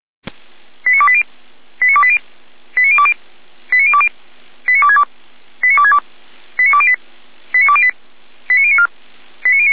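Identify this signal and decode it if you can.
Five-tone selective call (ZVEI), heard as quick runs of five tones, tones of about 70 ms. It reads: 89280 89280 89020 89020 89262 89262 89208 89208 89094
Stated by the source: ZVEI five-tone selective call received on SDR.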